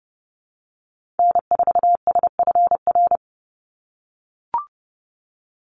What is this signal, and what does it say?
Morse code, heard as a keyed tone, about 30 wpm, 700 Hz. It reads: N4SFR
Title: NA SOTA Activator Callsigns 30wpm